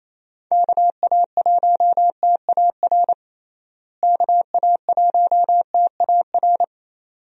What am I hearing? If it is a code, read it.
Morse code, heard as a keyed tone, about 27 wpm, 700 Hz. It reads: KA1TAR KA1TAR
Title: USA Call Signs - 6 Characters - Repeated 2x - 28wpm